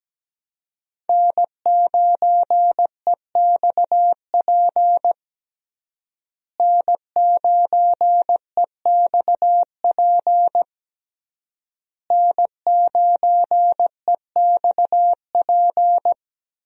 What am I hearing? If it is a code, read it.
Morse code, heard as a keyed tone, about 17 wpm, 700 Hz. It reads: N9EXP N9EXP N9EXP
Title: USA Call Signs - 5 Characters - Repeated 3x - 17wpm